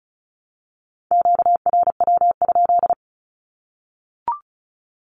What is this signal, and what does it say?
Morse code, heard as a keyed tone, about 34 wpm, 700 Hz. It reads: QRW?
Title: Q-Codes Repeated 3x 35wpm